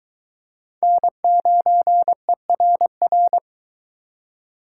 Morse code, heard as a keyed tone, about 22 wpm, 700 Hz. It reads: N9ERR